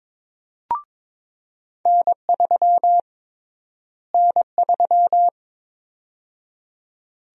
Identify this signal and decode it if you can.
Morse code, heard as a keyed tone, about 22 wpm, 700 Hz. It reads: N3 N3
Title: USA Call Sign Prefixes - Repeated 2x - 22wpm